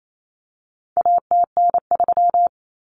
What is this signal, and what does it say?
Morse code, heard as a keyed tone, about 28 wpm, 700 Hz. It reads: ATN3
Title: Common Cut Numbers - 4-Digit 28wpm